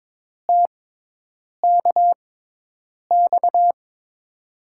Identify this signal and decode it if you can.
Morse code, heard as a keyed tone, about 22 wpm, spaced as for 8 wpm, 700 Hz. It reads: TKX
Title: USA Call Signs - 5 Characters - 6x Character Spacing - 22wpm